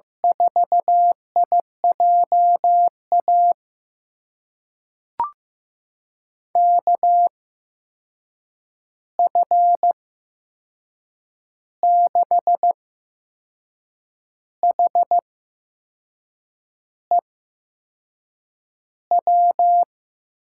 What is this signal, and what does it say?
Morse code, heard as a keyed tone, about 16 wpm, spaced as for 6 wpm, 700 Hz. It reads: 4IJA KF6HEW
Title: USA Call Signs - 6 Characters - 8x Character Spacing - 15wpm